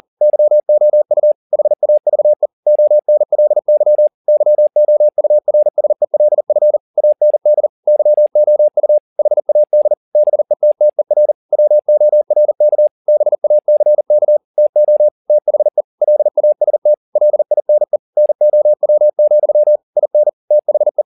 Morse code, heard as a keyed tone, about 40 wpm, 600 Hz. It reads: YOU HAVE ONLY YOURSELF AND YOU HAD BETTER WORK BACK TO THE LAST LINE NOW, IN THE